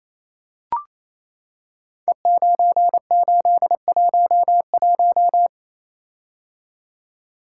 Morse code, heard as a keyed tone, about 27 wpm, 700 Hz. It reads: E9811
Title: Code Groups - 5 Letter:Numbers - 28wpm